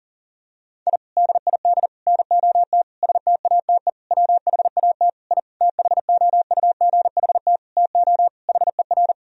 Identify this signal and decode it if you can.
Morse code, heard as a keyed tone, about 40 wpm, 700 Hz. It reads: I DID NOT STATE WHAT I THOUGHT TO HER
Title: Sentences from Top 1000 Words - Review Part 2 - 40wpm